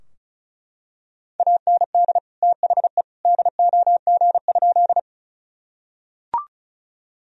Morse code, heard as a keyed tone, about 34 wpm, 700 Hz. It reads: AND THE DOG?
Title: Sentences from Top 1000 Words - Review Part 2 - 35wpm